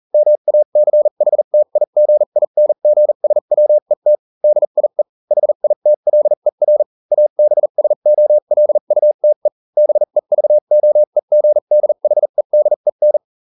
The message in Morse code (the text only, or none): MACHTIGINGSWET DIE HITLER ABSOLUTE BEVOEGDHEDEN